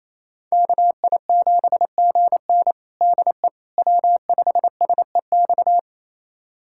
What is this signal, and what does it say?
Morse code, heard as a keyed tone, about 28 wpm, 700 Hz. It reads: KI7GN DE W5SEX
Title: USA Callsign DE Callsign - 5 Characters - 28wpm